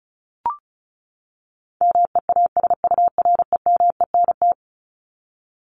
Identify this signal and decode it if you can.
Morse code, heard as a keyed tone, about 34 wpm, 700 Hz. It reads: MEASUREMENT
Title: Top 1751-2000 Words 35wpm